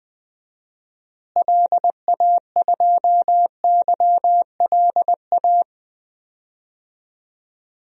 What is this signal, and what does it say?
Morse code, heard as a keyed tone, about 20 wpm, 700 Hz. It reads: LA2YLA
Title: DXCC-100 Call Signs - 6 Characters - Repeated 3x - 20wpm